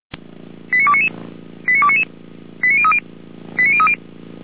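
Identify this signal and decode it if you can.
Five-tone selective call (ZVEI), heard as quick runs of five tones, tones of about 70 ms. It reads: 89299 89299 89030 89030
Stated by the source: ZVEI five-tone selective call received on an SDR by feuerwehr-orlamuende (Orlamünde fire brigade).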